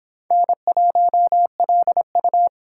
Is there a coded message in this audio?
Yes: Morse code from 0.3 to 2.5 s, heard as a keyed tone, about 31 wpm, 700 Hz. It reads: N1LU